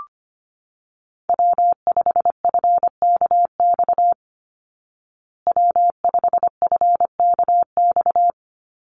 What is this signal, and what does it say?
Morse code, heard as a keyed tone, about 25 wpm, 700 Hz. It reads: W5FKX W5FKX
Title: Call Signs Repeated 3x - 25wpm